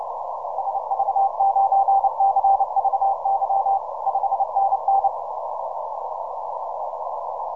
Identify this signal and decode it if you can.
Morse code, heard as a keyed tone, about 30 wpm, 780 Hz. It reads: SV9CVY 5NN
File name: simple-file-list / TL8TT 10MCW